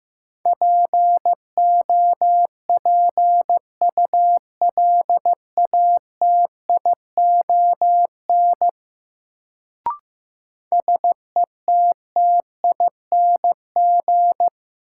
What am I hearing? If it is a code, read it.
Morse code, heard as a keyed tone, about 16 wpm, 700 Hz. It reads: POPULATION SETTING
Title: Sentences from Top 1300 Words - New Word Drill Alt - 15wpm